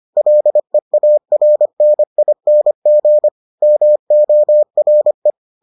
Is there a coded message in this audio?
Yes: Morse code from 0.2 to 5.3 s, heard as a keyed tone, about 25 wpm, 600 Hz. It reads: LEARNING MORE